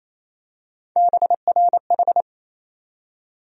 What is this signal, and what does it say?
Morse code, heard as a keyed tone, about 28 wpm, 700 Hz. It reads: BRH
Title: USA Call Sign Postfixes - 3 Letters 28wpm